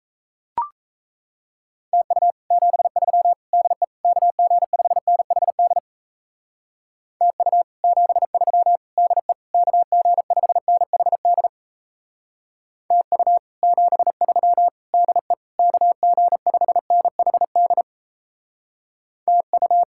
Morse code, heard as a keyed tone, about 37 wpm, 700 Hz. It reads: TU 73 DE KG5NHD TU 73 DE KG5NHD TU 73 DE KG5NHD TU